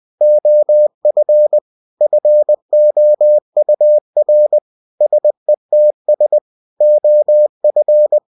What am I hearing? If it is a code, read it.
Morse code, heard as a keyed tone, about 20 wpm, 600 Hz. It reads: OF FOUR SETS OF
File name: cw-books / winnie-the-pooh / winnie-20wpm-page-0024